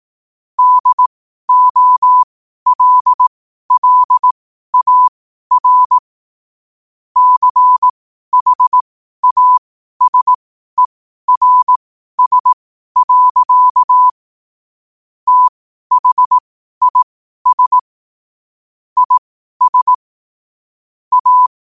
Morse code, heard as a keyed tone, about 18 wpm, spaced as for 12 wpm, 1000 Hz. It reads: DOLLAR CHASERS. THIS IS A
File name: qotdfiles / 12wpm